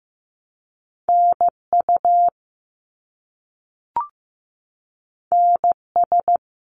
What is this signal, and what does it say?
Morse code, heard as a keyed tone, about 15 wpm, 700 Hz. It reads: NU NS
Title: Canadian Provinces and Territories - Repeated 3x - 15wpm